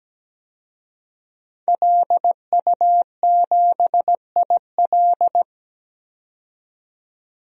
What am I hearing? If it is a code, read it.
Morse code, heard as a keyed tone, about 17 wpm, 700 Hz. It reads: LU7IL
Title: DXCC-100 Call Signs - 5 Characters - 17wpm